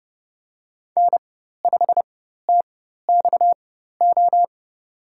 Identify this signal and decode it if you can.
Morse code, heard as a keyed tone, about 30 wpm, spaced as for 14 wpm, 700 Hz. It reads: N5TXO
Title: USA Call Signs - 5 Characters - 4x Character Spacing - 30wpm